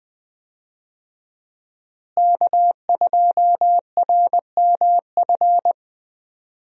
Morse code, heard as a keyed tone, about 20 wpm, 700 Hz. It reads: K2RMF
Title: USA Call Signs - 5 Characters - Repeated 2x - 20wpm